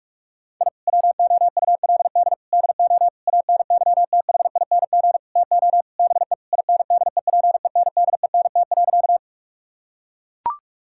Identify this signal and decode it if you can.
Morse code, heard as a keyed tone, about 41 wpm, 700 Hz. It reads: I WOULD DO ANYTHING TO BE INDEPENDENT.
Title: Sentences from Top 1000 Words - Review Part 6 - 45wpm